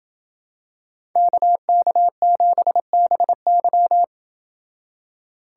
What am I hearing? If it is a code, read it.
Morse code, heard as a keyed tone, about 28 wpm, 700 Hz. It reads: KK7BY